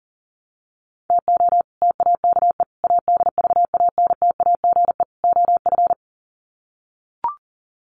Morse code, heard as a keyed tone, about 40 wpm, 700 Hz. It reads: TO TAKE ADVANTAGE OF
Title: Sets of 4 Words 40wpm